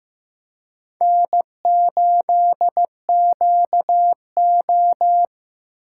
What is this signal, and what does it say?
Morse code, heard as a keyed tone, about 16 wpm, 700 Hz. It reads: N8QO
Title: USA Call Signs - 4 Characters - 3x Character Spacing - 15wpm